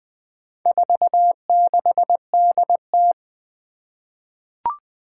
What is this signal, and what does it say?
Morse code, heard as a keyed tone, about 20 wpm, 700 Hz. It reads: 46DT
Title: Four Challenging Letter-Number Combinations 20wpm